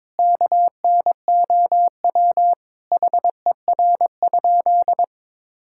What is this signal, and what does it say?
Morse code, heard as a keyed tone, about 22 wpm, 700 Hz. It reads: KNOW HER?
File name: practice / 22wpm / Sentences with 3-Word Phrases - 22wpm